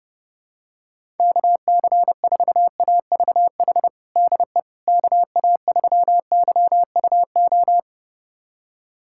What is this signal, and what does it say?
Morse code, heard as a keyed tone, about 32 wpm, 700 Hz. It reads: KC4AVH DE KA3YUO
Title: USA Callsign DE Callsign - 6 Characters - 30wpm